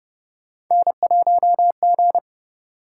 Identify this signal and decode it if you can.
Morse code, heard as a keyed tone, about 32 wpm, 700 Hz. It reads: N1G